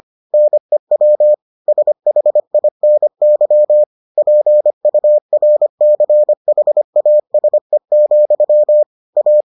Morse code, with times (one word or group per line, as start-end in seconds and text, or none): 0.3-1.3 NEW
1.7-3.8 SHINY
4.2-8.8 PURCHASE,
9.2-9.4 A